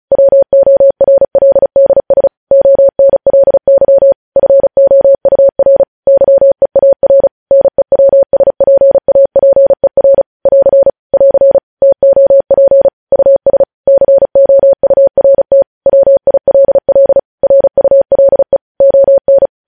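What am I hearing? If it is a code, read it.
Morse code, heard as a keyed tone, about 36 wpm, 550 Hz. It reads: WORLDS ONLY FOUR YEAR NEWSPAPER <AR> <AR> TOP US COURT WILL RULE ON
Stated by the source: Morse code news headlines sent at 35 wpm.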